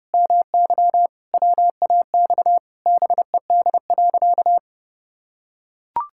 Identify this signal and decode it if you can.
Morse code, heard as a keyed tone, about 30 wpm, 700 Hz. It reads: MY WAX BED.
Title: Pangram Challenge v2 - 30wpm